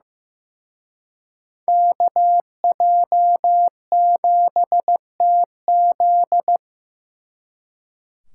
Morse code, heard as a keyed tone, about 18 wpm, 700 Hz. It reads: KJ7TZ
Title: Speed Racing - USA 2x2 Call Signs 15wpm